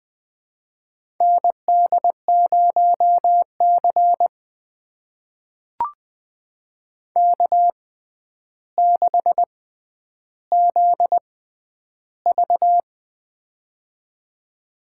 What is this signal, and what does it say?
Morse code, heard as a keyed tone, about 20 wpm, spaced as for 10 wpm, 700 Hz. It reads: ND0C K6ZV
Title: USA Call Signs - 4 Characters - 6x Character Spacing - 20wpm